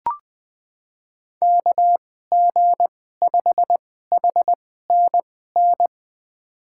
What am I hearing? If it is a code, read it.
Morse code, heard as a keyed tone, about 20 wpm, spaced as for 14 wpm, 700 Hz. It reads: KG5HNN
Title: USA Call Signs - 6 Characters - 2x Character Spacing - 20wpm